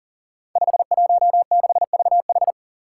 Morse code, heard as a keyed tone, about 37 wpm, 700 Hz. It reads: 516VH
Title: Five Challenging Letter-Number Combinations 40wpm